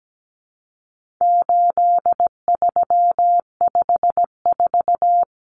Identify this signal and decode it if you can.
Morse code, heard as a keyed tone, about 17 wpm, 700 Hz. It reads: 8354
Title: Numbers - 4-Digit 17wpm